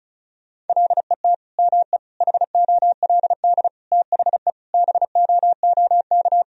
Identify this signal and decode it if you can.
Morse code, heard as a keyed tone, about 36 wpm, 700 Hz. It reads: LET ME HOLD THE BOOK
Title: Sentences from Top 1000 Words - Review Part 2 - 35wpm